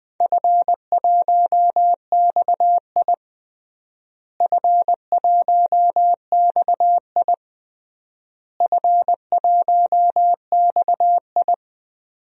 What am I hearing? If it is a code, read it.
Morse code, heard as a keyed tone, about 21 wpm, 700 Hz. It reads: F1XI F1XI F1XI
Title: DXCC-100 Call Signs - 4 Characters - Repeated 3x - 20wpm